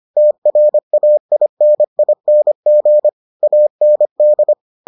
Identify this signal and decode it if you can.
Morse code, heard as a keyed tone, about 25 wpm, 600 Hz. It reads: TRAINING AND